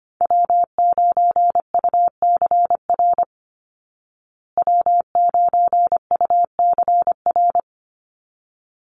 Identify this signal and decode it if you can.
Morse code, heard as a keyed tone, about 25 wpm, 700 Hz. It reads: W9UCR W9UCR
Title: Call Signs Repeated 3x - 25wpm